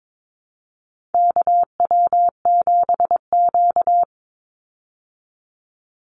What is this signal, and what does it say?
Morse code, heard as a keyed tone, about 22 wpm, 700 Hz. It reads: KW7Q
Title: Call Signs Repeated 3x - 22wpm